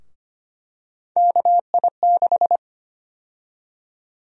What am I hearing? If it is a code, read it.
Morse code, heard as a keyed tone, about 25 wpm, 700 Hz. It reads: KI6
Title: USA Call Sign Prefixes - 25wpm